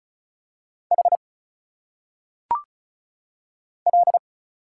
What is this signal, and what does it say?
Morse code, heard as a keyed tone, about 36 wpm, 700 Hz. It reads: H L
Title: Single Letters - Phonetic Pronunciation 35wpm